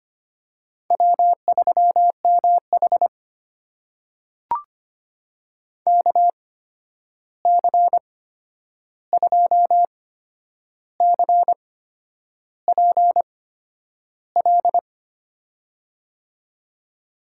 Morse code, heard as a keyed tone, about 25 wpm, spaced as for 9 wpm, 700 Hz. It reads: W3MH KC2CPL
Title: USA Call Signs - 8x Character Spacing - 25wpm